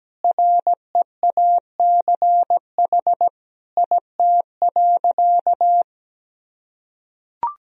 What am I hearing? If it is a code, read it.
Morse code, heard as a keyed tone, about 17 wpm, 700 Hz. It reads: REACH IT.
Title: Sentences from Top 400 Words and Abbreviations 17wpm